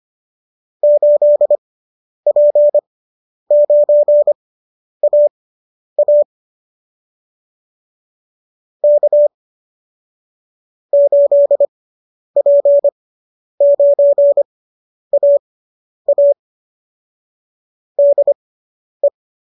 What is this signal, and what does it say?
Morse code, heard as a keyed tone, about 25 wpm, spaced as for 10 wpm, 570 Hz. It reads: 8P9AA K 8P9AA DE